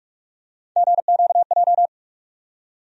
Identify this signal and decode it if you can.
Morse code, heard as a keyed tone, about 48 wpm, 700 Hz. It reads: GQJ
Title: USA Call Sign Postfixes - 3 Letters 45wpm